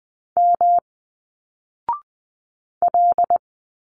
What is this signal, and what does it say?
Morse code, heard as a keyed tone, about 20 wpm, 700 Hz. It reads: M L
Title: Single Letters 20wpm